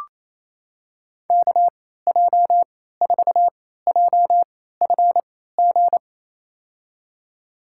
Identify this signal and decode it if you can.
Morse code, heard as a keyed tone, about 28 wpm, spaced as for 16 wpm, 700 Hz. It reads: KJ4JFG
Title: USA Call Signs - 6 Characters - 3x Character Spacing - 28wpm